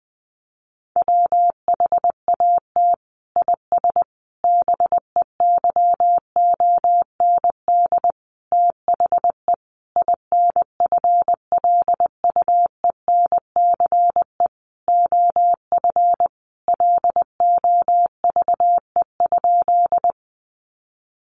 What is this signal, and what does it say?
Morse code, heard as a keyed tone, about 20 wpm, 700 Hz. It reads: WHAT IS BEYOND THE INFLUENCE OF LOVE?
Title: Sentences from Top 1100 Words 20wpm